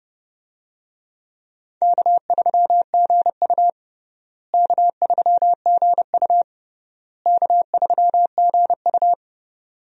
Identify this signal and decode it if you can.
Morse code, heard as a keyed tone, about 32 wpm, 700 Hz. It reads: K3GU K3GU K3GU
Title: USA Call Signs - 4 Characters - Repeated 3x - 30wpm